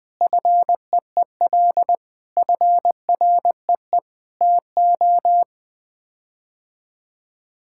Morse code, heard as a keyed tone, about 21 wpm, 700 Hz. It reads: FEEL FREE TO
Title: Sets of 8 Words - 20wpm